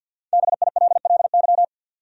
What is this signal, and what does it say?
Morse code, heard as a keyed tone, about 53 wpm, 700 Hz. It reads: BILLY
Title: USA Common Names - Max 5 Letters - 50wpm